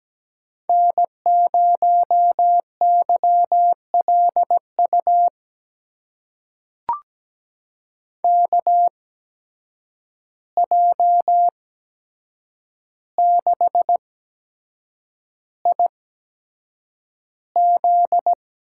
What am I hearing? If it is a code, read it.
Morse code, heard as a keyed tone, about 17 wpm, spaced as for 7 wpm, 700 Hz. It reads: N0YLU KJ6IZ
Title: USA Call Signs - 8x Character Spacing - 17wpm